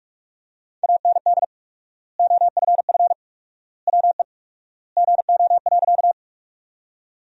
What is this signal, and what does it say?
Morse code, heard as a keyed tone, about 47 wpm, 700 Hz. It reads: AND OFF WE GO.
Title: 4x Word-Spacing - Sentences from Top 200 Words - 45wpm